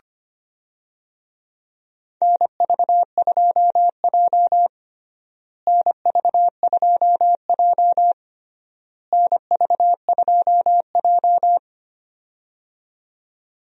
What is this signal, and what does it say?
Morse code, heard as a keyed tone, about 25 wpm, 700 Hz. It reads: NV2J NV2J NV2J
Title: USA Call Signs - 4 Characters - Repeated 3x - 25wpm